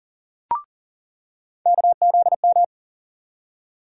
Morse code, heard as a keyed tone, about 40 wpm, 700 Hz. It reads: KZM
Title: USA Call Sign Postfixes - 3 Letters 40wpm